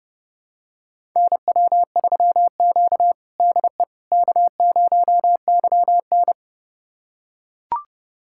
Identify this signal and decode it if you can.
Morse code, heard as a keyed tone, about 32 wpm, 700 Hz. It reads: NW3Q DE K0YN
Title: USA Callsign DE Callsign - 4 Characters - 30wpm